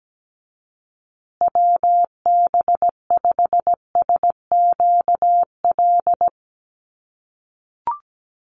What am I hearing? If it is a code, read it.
Morse code, heard as a keyed tone, about 17 wpm, 700 Hz. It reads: WB5SQL
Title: USA Call Signs Repeated 3x 17wpm